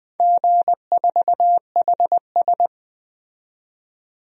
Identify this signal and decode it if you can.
Morse code, heard as a keyed tone, about 20 wpm, 700 Hz. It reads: G4HS
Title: DXCC-100 Call Signs - 4 Characters - 20wpm